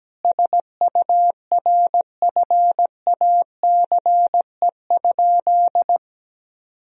Morse code, heard as a keyed tone, about 17 wpm, 700 Hz. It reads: SURFACE?
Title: Sentences from Top 1000 Words - Review Part 3 - 17wpm